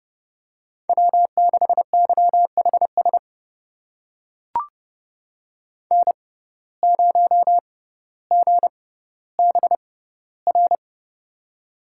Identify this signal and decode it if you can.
Morse code, heard as a keyed tone, about 32 wpm, spaced as for 16 wpm, 700 Hz. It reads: W6YHS N0GBR
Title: USA Call Signs - 5 Characters - 6x Character Spacing - 30wpm